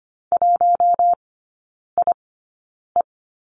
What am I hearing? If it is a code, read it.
Morse code, heard as a keyed tone, about 25 wpm, spaced as for 9 wpm, 690 Hz. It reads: 1IE
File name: Lesson 14e - Number 2 - Three Characters 25wpm